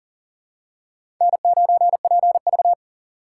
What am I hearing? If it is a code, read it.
Morse code, heard as a keyed tone, about 37 wpm, 700 Hz. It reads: N9PV
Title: USA Call Signs - 4 Characters - 4x Character Spacing - 40wpm